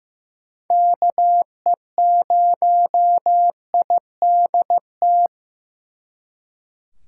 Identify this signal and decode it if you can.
Morse code, heard as a keyed tone, about 18 wpm, 700 Hz. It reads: KE0IDT
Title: Speed Racing - USA 2x3 Call Signs 15wpm